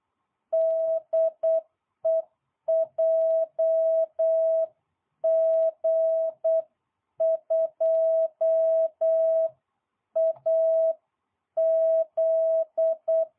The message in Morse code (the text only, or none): DE JG2AZ